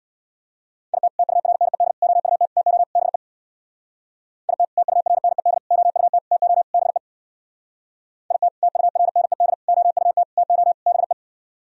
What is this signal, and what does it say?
Morse code, heard as a keyed tone, about 64 wpm, 700 Hz. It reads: IT TURNED OUT TO BE IT TURNED OUT TO BE IT TURNED OUT TO BE